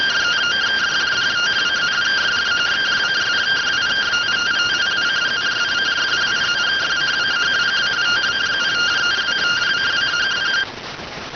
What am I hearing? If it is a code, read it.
Two-tone teleprinter signal, about 170 Hz shift.